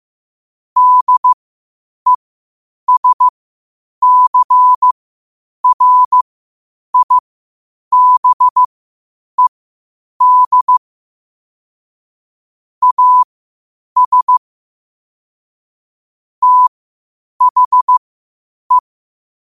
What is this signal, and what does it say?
Morse code, heard as a keyed tone, about 15 wpm, spaced as for 8 wpm, 1000 Hz. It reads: DESCRIBED AS THE
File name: qotdfiles / 8wpm